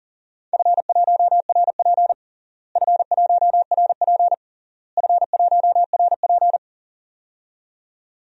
Morse code, heard as a keyed tone, about 40 wpm, 700 Hz. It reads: F1RP F1RP F1RP
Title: DXCC-100 Call Signs - 4 Characters - Repeated 3x - 40wpm